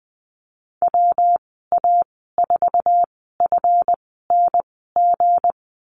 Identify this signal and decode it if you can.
Morse code, heard as a keyed tone, about 20 wpm, spaced as for 14 wpm, 700 Hz. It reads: WA4FNG
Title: 193 Lesson 31g - Letter v - Call Signs 20wpm